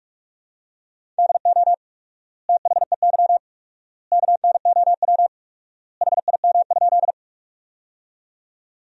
Morse code, heard as a keyed tone, about 41 wpm, 700 Hz. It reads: DO THEY KNOW HIM?
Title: 4x Word-Spacing - Sentences from Top 100 Words - 45wpm